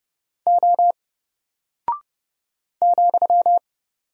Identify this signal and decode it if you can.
Morse code, heard as a keyed tone, about 30 wpm, 700 Hz. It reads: O ,
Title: Single Character (Letter:Number:Punctuation:Prosign) 30wpm